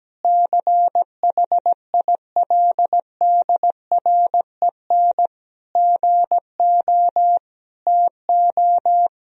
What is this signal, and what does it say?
Morse code, heard as a keyed tone, about 17 wpm, 700 Hz. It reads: CHILDREN GO TO
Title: Sentences from Top 1000 Words - Review Part 4 - 17wpm